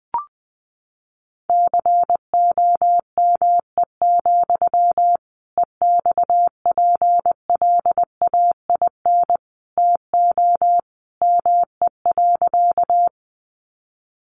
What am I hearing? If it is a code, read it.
Morse code, heard as a keyed tone, about 20 wpm, 700 Hz. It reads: COME, EXPLAIN TO ME.
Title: Sentences from Top 1100 Words 20wpm